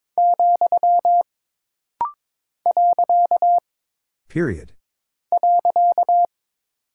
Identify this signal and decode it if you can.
Morse code, heard as a keyed tone, about 22 wpm, 700 Hz. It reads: , . .
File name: practice / 22wpm / Single Punctuation-Prosign - Rapid-Fire - 22wpm